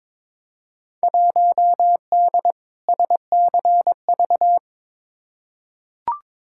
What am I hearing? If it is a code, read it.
Morse code, heard as a keyed tone, about 22 wpm, 700 Hz. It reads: 1D SCV
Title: Field Day Exchanges Encore - 22wpm